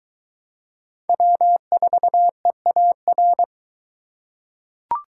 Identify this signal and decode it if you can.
Morse code, heard as a keyed tone, about 22 wpm, 700 Hz. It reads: W4EAR